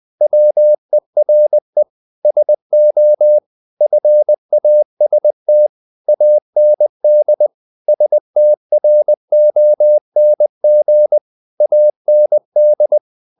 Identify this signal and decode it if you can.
Morse code, heard as a keyed tone, about 20 wpm, 600 Hz. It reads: WERE SO FAST AND STRONG AND
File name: the old man and the sea part 2 20wpm